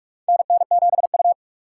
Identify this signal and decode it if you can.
Morse code, heard as a keyed tone, about 41 wpm, 700 Hz. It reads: NN7U
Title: USA Call Signs - 4 Characters - 4x Character Spacing - 45wpm